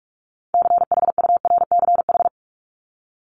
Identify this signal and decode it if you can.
Morse code, heard as a keyed tone, about 44 wpm, 700 Hz. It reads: CHURCH
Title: Top 1751-2000 Words 45wpm